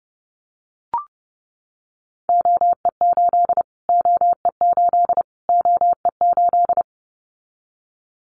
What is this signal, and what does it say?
Morse code, heard as a keyed tone, about 30 wpm, 700 Hz. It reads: OE8 OE8 OE8
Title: Callsign Prefixes for Top 25 DXCC - Repeated 3x - 30wpm